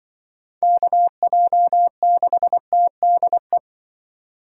Morse code, heard as a keyed tone, about 24 wpm, 700 Hz. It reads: KJ6TDE
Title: Speed Racing - USA 2x3 Call Signs 20wpm